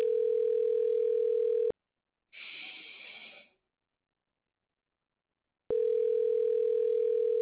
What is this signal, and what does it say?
Ringback tone (North America), 440 + 480 Hz.